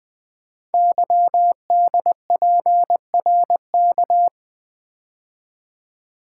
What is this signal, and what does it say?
Morse code, heard as a keyed tone, about 20 wpm, 700 Hz. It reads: YDPRK
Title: Code Groups - 5 Letters - 20wpm